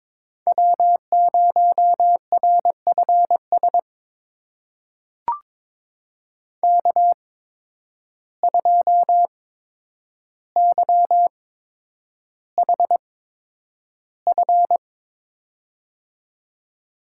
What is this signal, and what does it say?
Morse code, heard as a keyed tone, about 22 wpm, spaced as for 9 wpm, 700 Hz. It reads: W0RFS K2YHF
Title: USA Call Signs - 5 Characters - 8x Character Spacing - 22wpm